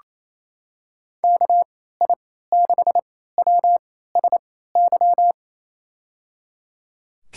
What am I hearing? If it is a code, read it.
Morse code, heard as a keyed tone, about 28 wpm, spaced as for 16 wpm, 700 Hz. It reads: KI6WSY
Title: USA Call Signs - 6 Characters - 3x Character Spacing - 28wpm